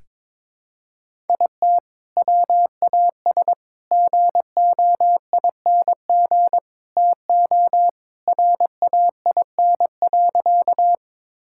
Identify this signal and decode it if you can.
Morse code, heard as a keyed tone, about 21 wpm, 700 Hz. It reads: IT WAS GOING TO RAIN.
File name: practice / 22wpm / Sentences with 4-Word Phrases - 22wpm